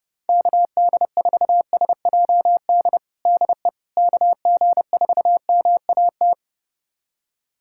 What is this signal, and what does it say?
Morse code, heard as a keyed tone, about 30 wpm, 700 Hz. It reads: KD4SJD DE KG4MAT
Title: USA Callsign DE Callsign 30wpm